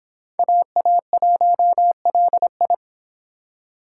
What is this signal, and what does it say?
Morse code, heard as a keyed tone, about 22 wpm, 700 Hz. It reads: AA1LI